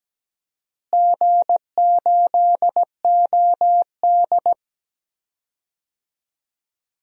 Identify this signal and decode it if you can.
Morse code, heard as a keyed tone, about 17 wpm, 700 Hz. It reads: G8OD
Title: DXCC-100 Call Signs - 4 Characters - Repeated 3x - 17wpm